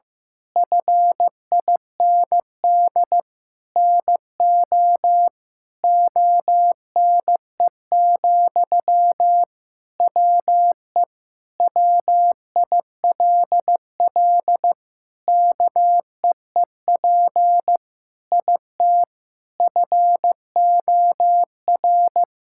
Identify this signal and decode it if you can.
Morse code, heard as a keyed tone, about 15 wpm, 700 Hz. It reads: FIND NO ONE, WE WILL KEEP IT FOR
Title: Sentences from Top 1600 Words 15wpm